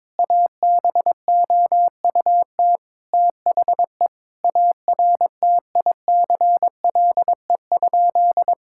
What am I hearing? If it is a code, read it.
Morse code, heard as a keyed tone, about 22 wpm, 700 Hz. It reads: ABOUT THE ARTICLE?
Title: Sentences from Top 1000 Words - Review Part 5 - 22wpm